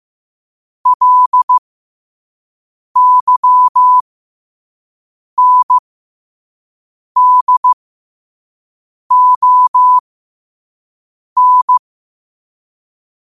Morse code, heard as a keyed tone, about 15 wpm, spaced as for 5 wpm, 1000 Hz. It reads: LYNDON
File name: qotdfiles / 5wpm